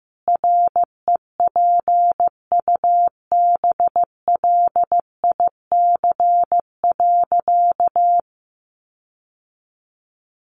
Morse code, heard as a keyed tone, about 15 wpm, 700 Hz. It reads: REPUBLIC.
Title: Sentences from Top 1500 Words 15wpm